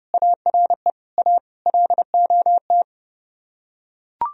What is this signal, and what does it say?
Morse code, heard as a keyed tone, about 30 wpm, 700 Hz. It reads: ARE A LOT